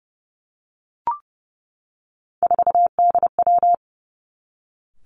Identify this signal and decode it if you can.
Morse code, heard as a keyed tone, about 30 wpm, 700 Hz. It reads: 4DW
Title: Three Challenging Letter-Number Combinations 30wpm